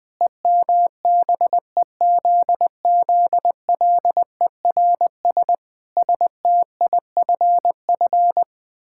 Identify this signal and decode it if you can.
Morse code, heard as a keyed tone, about 20 wpm, 700 Hz. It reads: EMBEZZLERS STIFF
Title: Pangram Challenge v2 - 20wpm